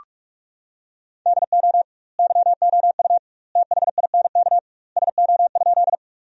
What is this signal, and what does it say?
Morse code, heard as a keyed tone, about 46 wpm, 700 Hz. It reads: DO YOU THINK SO?
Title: 2x Word-Spacing - Sentences from Top 200 Words - 45wpm